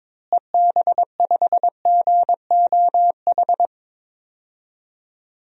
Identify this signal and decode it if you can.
Morse code, heard as a keyed tone, about 22 wpm, 700 Hz. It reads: EB5GOH
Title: DXCC-100 Call Signs - 6 Characters - Repeated 2x - 22wpm